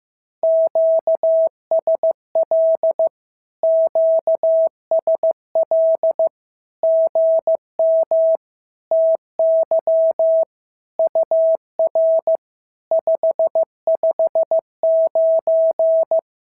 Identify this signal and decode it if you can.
Morse code, heard as a keyed tone, about 15 wpm, 650 Hz. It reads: QSL QSL GM TY UR 559
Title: SOTA QSO Practice 15wpm